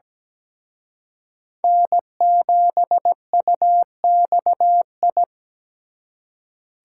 Morse code, heard as a keyed tone, about 20 wpm, 700 Hz. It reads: N7UXI
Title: Speed Racing - USA Call Signs 17wpm